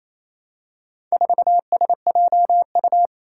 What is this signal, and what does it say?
Morse code, heard as a keyed tone, about 28 wpm, 700 Hz. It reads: 4SJU
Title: Four Challenging Letter-Number Combinations 28wpm